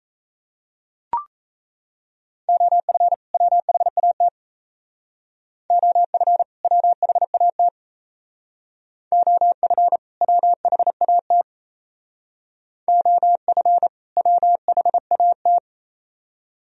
Morse code, heard as a keyed tone, about 32 wpm, 690 Hz. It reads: OF WHAT OF WHAT OF WHAT OF WHAT
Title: Speed-Racing - Sets of 2 Words 28wpm